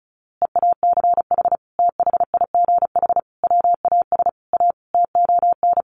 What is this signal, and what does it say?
Morse code, heard as a keyed tone, about 34 wpm, 700 Hz. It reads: EACH THIGH WAS A TON
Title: A Sound Of Thunder 35wpm